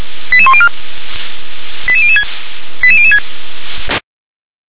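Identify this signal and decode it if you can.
Five-tone selective call (ZVEI), heard as quick runs of five tones, tones of about 70 ms. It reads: 88104 88006 88006
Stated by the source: Received by SDR, ZVEI five-tone call 88101.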